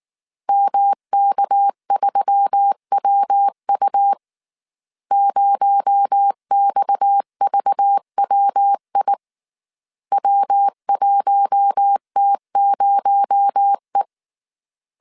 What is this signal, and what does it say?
Morse code, heard as a keyed tone, about 19 wpm, 800 Hz. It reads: MK3WU 0XVWI W1T0E
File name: CW / DET / 1 6 1500C S A 18wpm 2255